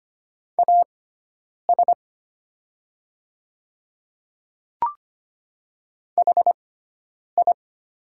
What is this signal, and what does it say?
Morse code, heard as a keyed tone, about 25 wpm, spaced as for 9 wpm, 690 Hz. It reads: AS HI
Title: 6x Character Spacing - Top 200 Words - 25wpm